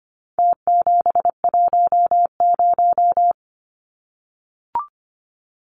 Morse code, heard as a keyed tone, about 25 wpm, 700 Hz. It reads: T710
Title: Numbers - Contest Style - 4-Digit 25wpm